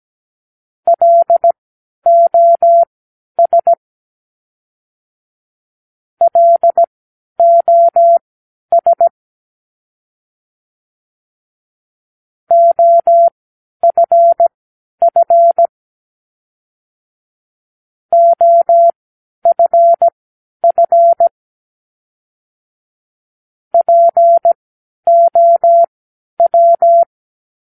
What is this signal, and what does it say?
Morse code, heard as a keyed tone, about 17 wpm, spaced as for 10 wpm, 680 Hz. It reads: LOS LOS OFF OFF POW
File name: k7qo / Practicewords